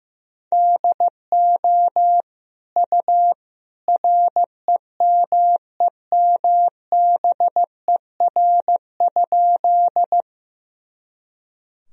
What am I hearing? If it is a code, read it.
Morse code, heard as a keyed tone, about 15 wpm, 700 Hz. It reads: DO U REMEMBER?
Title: Sentences from Top 400 Words and Abbreviations 15wpm